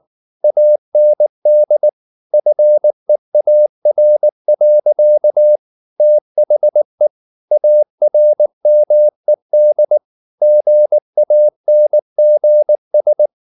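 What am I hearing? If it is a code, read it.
Morse code, heard as a keyed tone, about 19 wpm, 600 Hz. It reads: AND FEAR. THE ARMED GANGS
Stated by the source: Morse code at 19 wpm from CW med Gnister, reading Ken Follett's novel Never in English.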